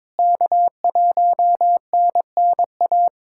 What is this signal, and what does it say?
Morse code, heard as a keyed tone, about 22 wpm, 700 Hz. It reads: K1NNA
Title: USA Callsign DE Callsign 22wpm